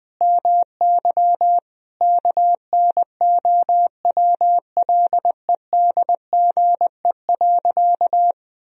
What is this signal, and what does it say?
Morse code, heard as a keyed tone, about 21 wpm, 700 Hz. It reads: MY KNOWLEDGE.
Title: Sentences from Top 1000 Words - Review Part 3 - 20wpm